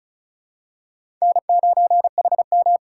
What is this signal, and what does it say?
Morse code, heard as a keyed tone, about 34 wpm, 700 Hz. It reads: N9HM
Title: USA Call Signs - 4 Characters - 4x Character Spacing - 35wpm